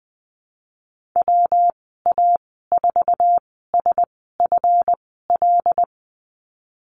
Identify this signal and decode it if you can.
Morse code, heard as a keyed tone, about 19 wpm, spaced as for 14 wpm, 700 Hz. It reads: WA4SFL
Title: 147 Lesson 24g - Letter y - Call Signs 20wpm